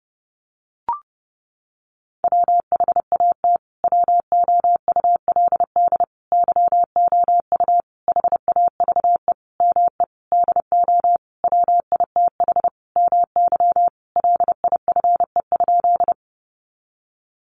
Morse code, heard as a keyed tone, about 30 wpm, 700 Hz. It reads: WHAT WOULD YOU HAVE ME DO WITH MY LIFE?
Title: Sentences from Top 300 Words 30wpm